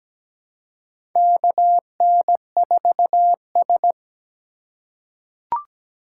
Morse code, heard as a keyed tone, about 17 wpm, 700 Hz. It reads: KN4S
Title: USA Call Signs - 4 Characters - 3x Character Spacing - 17wpm